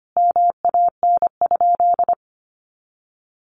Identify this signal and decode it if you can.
Morse code, heard as a keyed tone, about 25 wpm, 700 Hz. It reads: MAN?
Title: Sentences from Top 1300 Words 25wpm